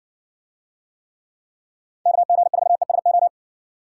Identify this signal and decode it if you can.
Morse code, heard as a keyed tone, about 60 wpm, 700 Hz. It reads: KG4ESO